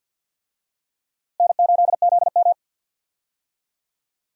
Morse code, heard as a keyed tone, about 52 wpm, 700 Hz. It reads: N8ZM
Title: USA Call Signs - 4 Characters - Repeated 3x - 50wpm